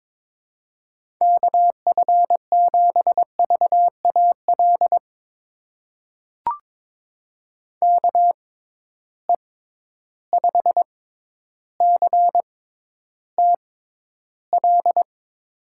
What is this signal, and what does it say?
Morse code, heard as a keyed tone, about 22 wpm, spaced as for 11 wpm, 700 Hz. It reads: KF7VAL KE5CTL
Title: USA Call Signs - 6 Characters - 6x Character Spacing - 22wpm